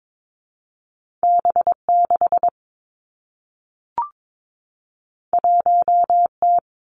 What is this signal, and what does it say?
Morse code, heard as a keyed tone, about 22 wpm, 700 Hz. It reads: B 6 1 T
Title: Two Challenging Letter-Number Combinations 22wpm